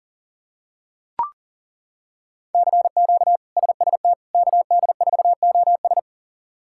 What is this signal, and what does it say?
Morse code, heard as a keyed tone, about 40 wpm, 700 Hz. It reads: CQ SST KD4OS
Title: K1USN SST QSO Practice - 40wpm